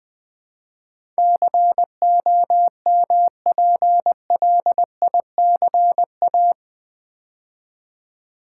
Morse code, heard as a keyed tone, about 21 wpm, 700 Hz. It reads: COMPLICA
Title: Build A Word - 4 Letters At A Time - 12-Letter Words - 20wpm